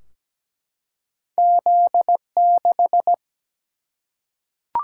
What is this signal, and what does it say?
Morse code, heard as a keyed tone, about 17 wpm, 700 Hz. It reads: Z6
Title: Single-Letter Single-Number 17wpm